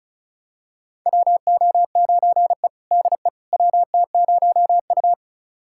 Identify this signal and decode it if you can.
Morse code, heard as a keyed tone, about 36 wpm, 700 Hz. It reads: WO9E DE WT0U
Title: USA Callsign DE Callsign - 4 Characters - 35wpm